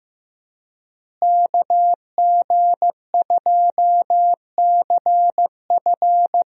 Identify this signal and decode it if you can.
Morse code, heard as a keyed tone, about 15 wpm, 700 Hz. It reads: KG2CF
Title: USA Call Signs - 5 Characters - Repeated 3x - 15wpm